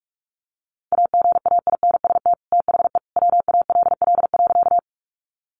Morse code, heard as a keyed tone, about 44 wpm, 700 Hz. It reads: AGAINST THE WALL.
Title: Sentences from Top 1100 Words 45wpm